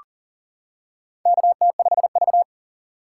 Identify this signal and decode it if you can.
Morse code, heard as a keyed tone, about 40 wpm, 690 Hz. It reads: KT5V
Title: K1USN SST QSO Practice - 40wpm